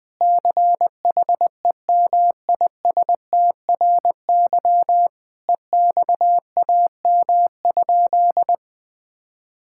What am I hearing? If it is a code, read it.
Morse code, heard as a keyed tone, about 21 wpm, 700 Hz. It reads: CHEMISTRY EXAM?
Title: Sentences from Top 1000 Words - Review Part 6 - 20wpm